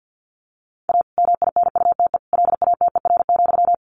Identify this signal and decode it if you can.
Morse code, heard as a keyed tone, about 50 wpm, 700 Hz. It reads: A MINUTE LATER,
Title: Sentences from Top 1500 Words 50wpm